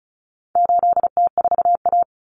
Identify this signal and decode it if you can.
Morse code, heard as a keyed tone, about 34 wpm, 700 Hz. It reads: 8T4A
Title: Common Cut Numbers - 4-Digit 35wpm